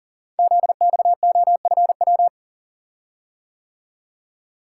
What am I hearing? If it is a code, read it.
Morse code, heard as a keyed tone, about 38 wpm, 690 Hz. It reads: ZXOFW
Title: Code Groups - 5 Letter:Numbers - 40wpm